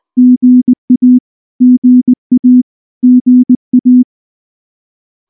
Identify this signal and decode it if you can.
Morse code, heard as a keyed tone, about 19 wpm, 250 Hz. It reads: GA GA GA
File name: morse / morse-training-20WPM-10